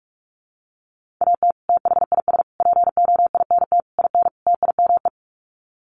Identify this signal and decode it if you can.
Morse code, heard as a keyed tone, about 44 wpm, 700 Hz. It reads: AT THIS POINT IN TIME
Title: Sets of 5 Words Encore 45wpm